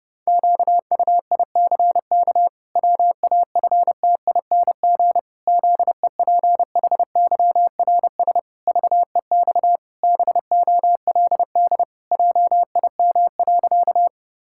Morse code, heard as a keyed tone, about 30 wpm, 700 Hz. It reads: QUICK WAFTING ZEPHYRS VEX BOLD JIM.
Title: Pangram Challenge v2 - 30wpm